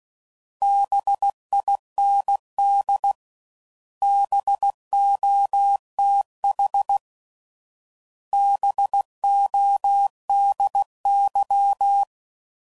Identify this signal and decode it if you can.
Morse code, heard as a keyed tone, about 16 wpm, 800 Hz. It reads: BIND BOTH BODY